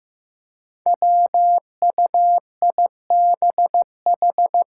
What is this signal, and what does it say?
Morse code, heard as a keyed tone, about 15 wpm, 700 Hz. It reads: WUIBH
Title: Five Challenging Letter-Number Combinations 15wpm